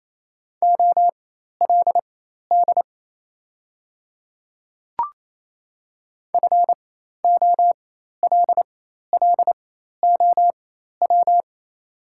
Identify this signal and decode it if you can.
Morse code, heard as a keyed tone, about 28 wpm, spaced as for 13 wpm, 700 Hz. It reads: OLD FOLLOW
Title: 4x Character Spacing - Top 300 Words - 28wpm